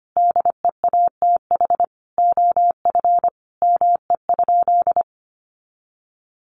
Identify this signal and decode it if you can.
Morse code, heard as a keyed tone, about 25 wpm, 700 Hz. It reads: DEATH OF ME?
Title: Sentences from Top 1100 Words 25wpm